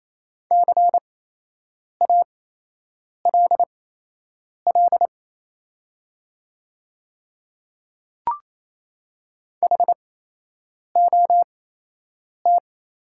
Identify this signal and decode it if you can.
Morse code, heard as a keyed tone, about 28 wpm, spaced as for 8 wpm, 700 Hz. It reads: CALL HOT
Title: 8x Character Spacing - Top 100 Words - 28wpm